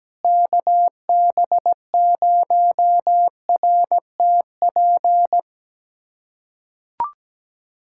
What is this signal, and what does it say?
Morse code, heard as a keyed tone, about 17 wpm, 700 Hz. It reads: KB0RTP
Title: USA Callsign DE Callsign - 6 Characters - 17wpm